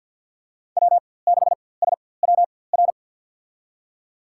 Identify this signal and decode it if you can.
Morse code, heard as a keyed tone, about 52 wpm, spaced as for 23 wpm, 700 Hz. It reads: W6SWR
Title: USA Call Signs - 5 Characters - 4x Character Spacing - 50wpm